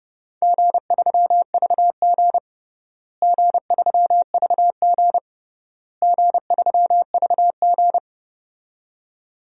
Morse code, heard as a keyed tone, about 30 wpm, 700 Hz. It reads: G3VG G3VG G3VG
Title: DXCC-100 Call Signs - 4 Characters - Repeated 3x - 30wpm